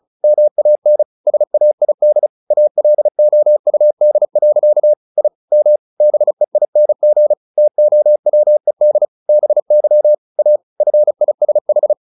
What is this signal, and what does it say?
Morse code, heard as a keyed tone, about 34 wpm, 600 Hz. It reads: MAN SAID ALOUD. I M BEING TOWED BY A FISH
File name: the old man and the sea / the old man and the sea 1 35wpm